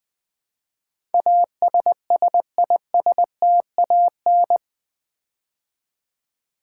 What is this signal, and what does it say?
Morse code, heard as a keyed tone, about 20 wpm, 700 Hz. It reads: ASSISTAN
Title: Build A Word - 4 Letters At A Time - 10-Letter Words - 20wpm